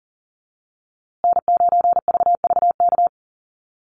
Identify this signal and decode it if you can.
Morse code, heard as a keyed tone, about 39 wpm, 700 Hz. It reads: N9VVK
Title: USA Call Signs Repeated 2x 40wpm